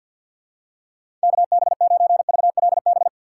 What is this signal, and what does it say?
Morse code, heard as a keyed tone, about 51 wpm, 700 Hz. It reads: KB9VLB
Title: USA Call Signs - 6 Characters - 4x Character Spacing - 50wpm